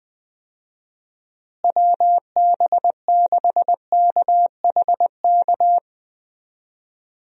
Morse code, heard as a keyed tone, about 20 wpm, 700 Hz. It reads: WB6KHK
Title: USA Call Signs - 6 Characters - Repeated 3x - 20wpm